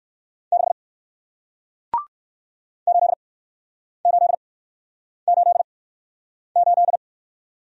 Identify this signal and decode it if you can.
Morse code, heard as a keyed tone, about 61 wpm, spaced as for 9 wpm, 700 Hz. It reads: 6 8888